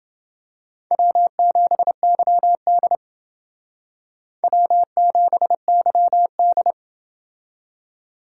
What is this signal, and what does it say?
Morse code, heard as a keyed tone, about 28 wpm, 700 Hz. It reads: W7YD W7YD